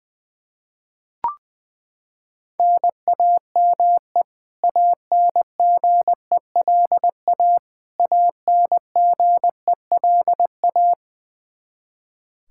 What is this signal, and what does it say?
Morse code, heard as a keyed tone, about 20 wpm, 700 Hz. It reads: NAME ANGELA ANGELA
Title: Practice QSO - Exchange 1 - Beginning - 20wpm